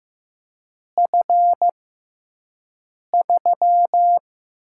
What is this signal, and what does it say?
Morse code, heard as a keyed tone, about 15 wpm, spaced as for 5 wpm, 700 Hz. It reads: F3
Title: USA Call Signs - 6x Character Spacing - 15wpm